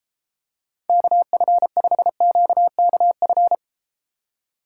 Morse code, heard as a keyed tone, about 30 wpm, 700 Hz. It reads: KF5QKF